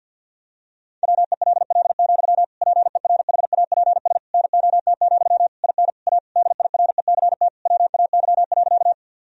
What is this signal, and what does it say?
Morse code, heard as a keyed tone, about 50 wpm, 700 Hz. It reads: WELL, PERHAPS NOT, IN A DIRECT WAY.
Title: Sentences from Top 1000 Words - Review Part 2 - 50wpm